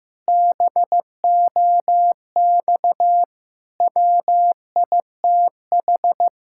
Morse code, heard as a keyed tone, about 15 wpm, 700 Hz. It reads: BOX WITH
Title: Pangram Challenge v2 - 15wpm